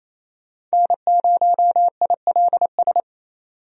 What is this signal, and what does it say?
Morse code, heard as a keyed tone, about 28 wpm, 700 Hz. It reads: N0ILS
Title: USA Call Signs - 5 Characters - 28wpm